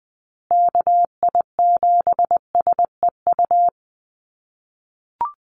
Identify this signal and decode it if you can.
Morse code, heard as a keyed tone, about 20 wpm, 700 Hz. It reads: KI7SEU
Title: USA Call Signs Repeated 3x 20wpm